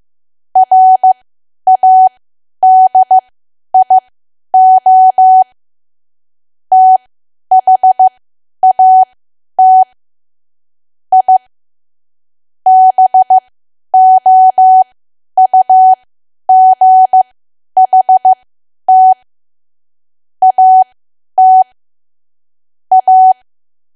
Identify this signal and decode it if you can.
Morse code, heard as a keyed tone, about 15 wpm, spaced as for 10 wpm, 750 Hz. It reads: RADIO THAT I BOUGHT AT A
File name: cwa / W1AW-Level2 / 10WPM